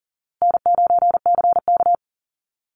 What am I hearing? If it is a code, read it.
Morse code, heard as a keyed tone, about 40 wpm, 700 Hz. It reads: N9CK
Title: Easy Call Signs Repeated 3x - 40wpm